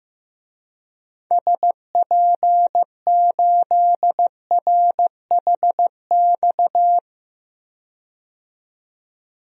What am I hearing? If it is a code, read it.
Morse code, heard as a keyed tone, about 15 wpm, 700 Hz. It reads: SP8RHX
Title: DXCC-100 Call Signs - 6 Characters - 15wpm